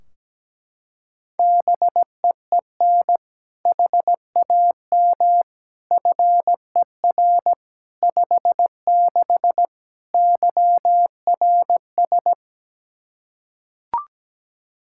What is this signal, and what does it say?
Morse code, heard as a keyed tone, about 17 wpm, 700 Hz. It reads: BEEN HAM FER 56 YRS
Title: Practice QSO - Exchange 3 - More Info - 17wpm